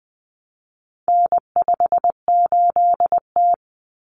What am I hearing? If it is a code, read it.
Morse code, heard as a keyed tone, about 20 wpm, 700 Hz. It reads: N58T
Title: Common Cut Numbers - 4-Digit 20wpm